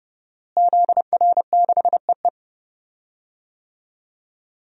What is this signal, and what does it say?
Morse code, heard as a keyed tone, about 30 wpm, 700 Hz. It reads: ZR6EE
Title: DXCC-100 Call Signs - 5 Characters - Repeated 2x - 30wpm